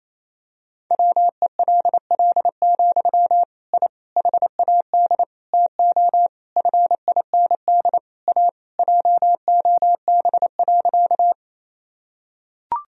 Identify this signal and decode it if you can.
Morse code, heard as a keyed tone, about 27 wpm, 700 Hz. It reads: WELL, I HAD TO FIND A JOB.
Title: Sentences from Top 1000 Words - Review Part 3 - 28wpm